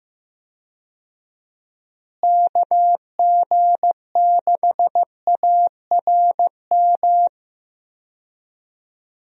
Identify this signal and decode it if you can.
Morse code, heard as a keyed tone, about 15 wpm, 700 Hz. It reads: KG6ARM
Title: USA Call Signs - 6 Characters - Repeated 2x - 15wpm